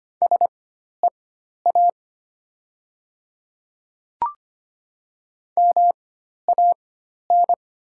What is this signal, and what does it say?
Morse code, heard as a keyed tone, about 25 wpm, spaced as for 12 wpm, 700 Hz. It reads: SEA MAN
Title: 4x Character Spacing - Top 500 Words - 25wpm